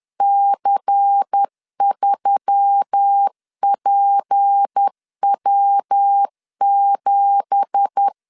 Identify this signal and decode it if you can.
Morse code, heard as a keyed tone, about 10 wpm, 800 Hz. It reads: C3PW7
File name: CW / DET / 1 9 1500C S B 10wpm 4439